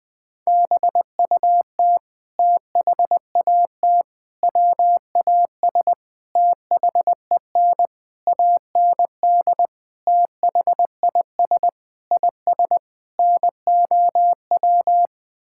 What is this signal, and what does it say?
Morse code, heard as a keyed tone, about 21 wpm, 700 Hz. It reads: BUT THAT WAS THEN AND THIS IS NOW
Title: Sets of 8 Words - 20wpm